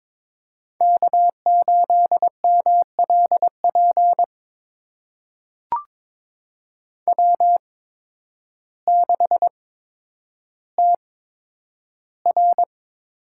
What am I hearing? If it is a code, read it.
Morse code, heard as a keyed tone, about 22 wpm, spaced as for 10 wpm, 700 Hz. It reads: K8MLP W6TR
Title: USA Call Signs - 5 Characters - 8x Character Spacing - 22wpm